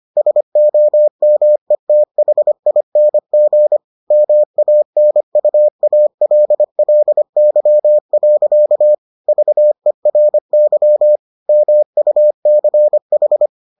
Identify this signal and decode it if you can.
Morse code, heard as a keyed tone, about 25 wpm, 600 Hz. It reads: SOMETHING MANUALLY. VERY MUCH